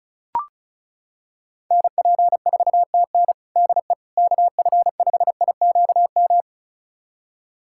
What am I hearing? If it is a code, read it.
Morse code, heard as a keyed tone, about 36 wpm, 700 Hz. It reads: NP4TN DE KF5IQM
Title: USA Callsign DE Callsign 35wpm